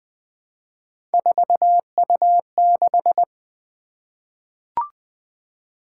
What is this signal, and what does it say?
Morse code, heard as a keyed tone, about 20 wpm, 700 Hz. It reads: 4U6
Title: Three Letter-Number Combinations - 20wpm